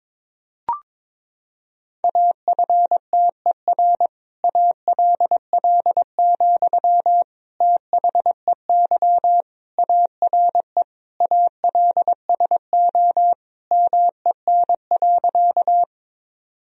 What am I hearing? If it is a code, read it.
Morse code, heard as a keyed tone, about 21 wpm, 700 Hz. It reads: AFTER ALL, THEY ARE ALSO MEN.
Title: Sentences from Top 1000 Words - Review Part 1 - 22wpm